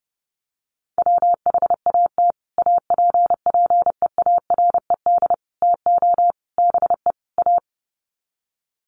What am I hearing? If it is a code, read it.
Morse code, heard as a keyed tone, about 30 wpm, 700 Hz. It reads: WHAT APPEARED TO BE A
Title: Sets of 5 Words Encore2 30wpm